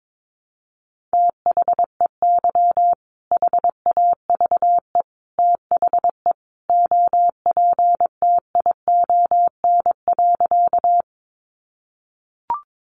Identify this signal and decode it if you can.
Morse code, heard as a keyed tone, about 22 wpm, 700 Hz. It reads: THEY HAVE THE OPTION.
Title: Sentences from Top 1300 Words 22wpm